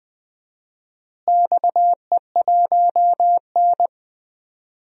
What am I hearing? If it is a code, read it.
Morse code, heard as a keyed tone, about 20 wpm, 700 Hz. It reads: XE1N
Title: DXCC-100 Call Signs - 4 Characters - Repeated 2x - 20wpm